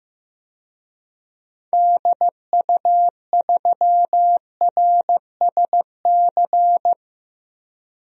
Morse code, heard as a keyed tone, about 15 wpm, 700 Hz. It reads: DU3RSC
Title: DXCC-100 Call Signs - 6 Characters - 15wpm